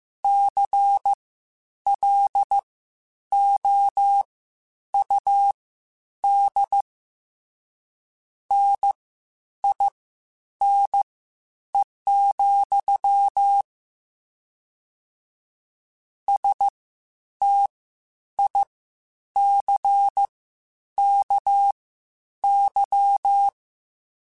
Morse code, heard as a keyed tone, about 15 wpm, spaced as for 9 wpm, 800 Hz. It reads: CLOUD NINE, STICKY
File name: Misc / Misc-mp3s / WordPairs sp @13